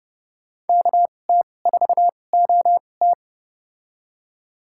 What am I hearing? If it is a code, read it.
Morse code, heard as a keyed tone, about 32 wpm, spaced as for 22 wpm, 700 Hz. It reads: KT4OT
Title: USA Call Signs - 5 Characters - 2x Character Spacing - 30wpm